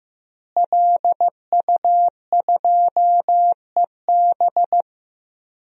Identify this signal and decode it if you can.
Morse code, heard as a keyed tone, about 15 wpm, 700 Hz. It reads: LU2EB
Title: DXCC-100 Call Signs - 5 Characters - Repeated 3x - 15wpm